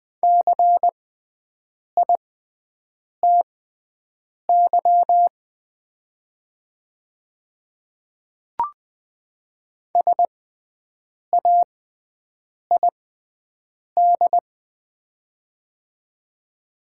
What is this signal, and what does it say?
Morse code, heard as a keyed tone, about 20 wpm, spaced as for 7 wpm, 700 Hz. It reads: CITY SAID
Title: 6x Character Spacing - Top 300 Words - 20wpm